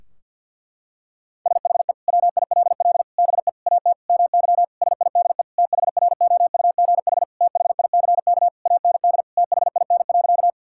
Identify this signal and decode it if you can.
Morse code, heard as a keyed tone, about 50 wpm, 700 Hz. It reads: SHE WILL BE AT MY SIDE THROUGH THICK AND THIN.
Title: Sentences from Top 1600 Words 50wpm